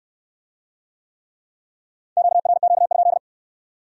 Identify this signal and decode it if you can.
Morse code, heard as a keyed tone, about 65 wpm, 700 Hz. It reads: QRQ?